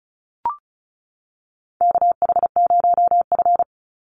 Morse code, heard as a keyed tone, about 34 wpm, 700 Hz. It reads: KH0F
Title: USA Call Signs 35wpm